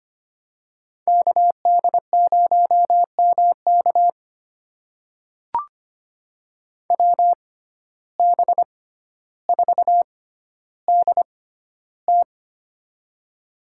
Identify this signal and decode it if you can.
Morse code, heard as a keyed tone, about 25 wpm, spaced as for 13 wpm, 700 Hz. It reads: KD0MK WB4DT
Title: USA Call Signs - 5 Characters - 6x Character Spacing - 25wpm